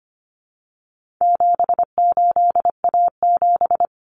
Morse code, heard as a keyed tone, about 25 wpm, 700 Hz. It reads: 78A7
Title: Common Cut Numbers - 4-Digit 25wpm